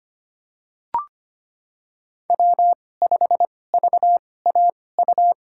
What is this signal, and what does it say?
Morse code, heard as a keyed tone, about 25 wpm, spaced as for 18 wpm, 700 Hz. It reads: W5VAU
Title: USA Call Signs - 2x Character Spacing - 25wpm